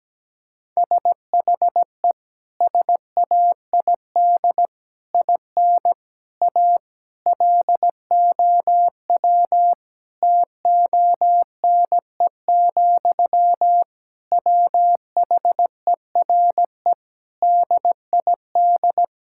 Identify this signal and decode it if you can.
Morse code, heard as a keyed tone, about 17 wpm, 700 Hz. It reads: SHE SAID IN A LOW TONE, WHERE DID
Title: Sentences from Top 1600 Words 17wpm